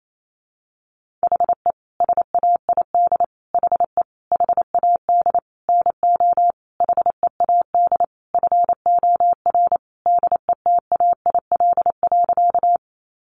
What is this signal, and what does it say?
Morse code, heard as a keyed tone, about 28 wpm, 700 Hz. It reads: HE SAID HE HAD NO HEAD FOR DETAIL.
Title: Sentences from Top 1500 Words 28wpm